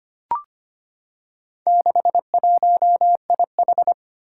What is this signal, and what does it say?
Morse code, heard as a keyed tone, about 25 wpm, 700 Hz. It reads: 61IH
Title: Four Challenging Letter-Number Combinations 25wpm